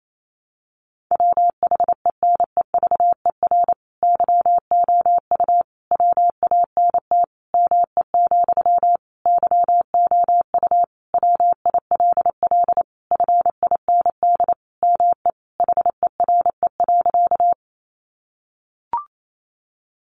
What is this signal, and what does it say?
Morse code, heard as a keyed tone, about 28 wpm, 700 Hz. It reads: WHENEVER YOU WANT ME, YOU WILL FIND ME HERE.
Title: Sentences from Top 1200 Words 28wpm